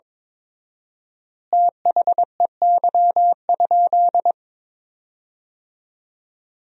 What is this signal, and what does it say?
Morse code, heard as a keyed tone, about 22 wpm, 700 Hz. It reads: THEY?
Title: 4x Word-Spacing - Sentences from Top 300 Words - 22wpm